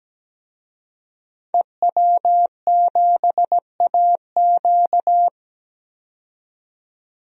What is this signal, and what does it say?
Morse code, heard as a keyed tone, about 17 wpm, 700 Hz. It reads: EW7AQ
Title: DXCC-100 Call Signs - 5 Characters - 17wpm